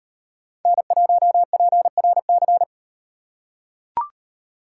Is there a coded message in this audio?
Yes: Morse code from 0.7 to 2.6 s, heard as a keyed tone, about 38 wpm, 700 Hz. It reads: N1PRC